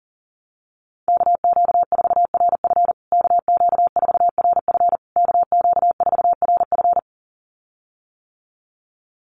Morse code, heard as a keyed tone, about 40 wpm, 700 Hz. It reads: KQ4RF KQ4RF KQ4RF
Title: USA Call Signs Repeated 3x 40wpm